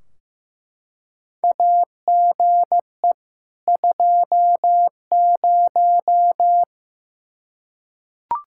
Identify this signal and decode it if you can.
Morse code, heard as a keyed tone, about 15 wpm, 700 Hz. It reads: AGE 20
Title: Practice QSO - Exchange 3 - More Info - 15wpm